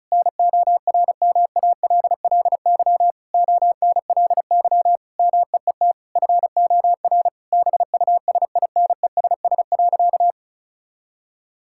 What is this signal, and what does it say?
Morse code, heard as a keyed tone, about 36 wpm, 700 Hz. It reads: NORMALLY ONLY MEET FOR BUSINESS.
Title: Sentences from Top 1000 Words - Review Part 6 - 35wpm